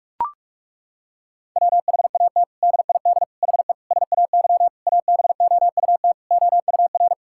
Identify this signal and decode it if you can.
Morse code, heard as a keyed tone, about 41 wpm, 700 Hz. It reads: WHAT DID HE SAY ABOUT OUR
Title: Sentences from Top 1000 Words - Review Part 6 - 45wpm